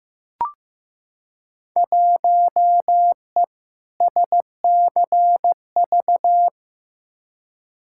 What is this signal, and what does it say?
Morse code, heard as a keyed tone, about 15 wpm, 700 Hz. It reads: 1E SCV
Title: Field Day Exchanges - 15wpm